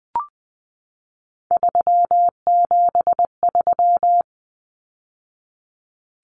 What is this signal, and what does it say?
Morse code, heard as a keyed tone, about 20 wpm, 700 Hz. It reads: 373
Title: Numbers - 3-Digit 20wpm